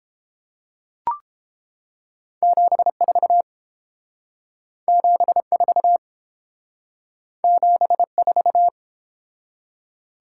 Morse code, heard as a keyed tone, about 30 wpm, 700 Hz. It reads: 74 74 74